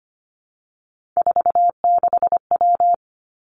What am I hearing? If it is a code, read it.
Morse code, heard as a keyed tone, about 25 wpm, 700 Hz. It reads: 46W
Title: Three Challenging Letter-Number Combinations 25wpm